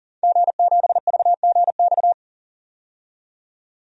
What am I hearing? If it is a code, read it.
Morse code, heard as a keyed tone, about 37 wpm, 700 Hz. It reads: G7VGX
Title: DXCC-100 Call Signs - 5 Characters - 40wpm